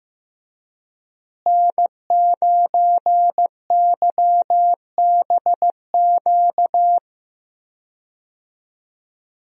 Morse code, heard as a keyed tone, about 15 wpm, 700 Hz. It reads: N9YBQ
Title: USA Call Signs - 5 Characters - Repeated 3x - 15wpm